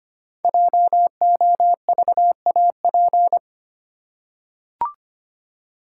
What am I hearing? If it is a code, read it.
Morse code, heard as a keyed tone, about 25 wpm, 700 Hz. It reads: JOVAP
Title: Code Groups - 5 Letters - 25wpm